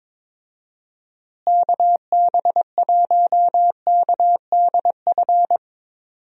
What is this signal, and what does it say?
Morse code, heard as a keyed tone, about 22 wpm, 700 Hz. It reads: KB1KDF
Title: USA Call Signs - 6 Characters - Repeated 3x - 22wpm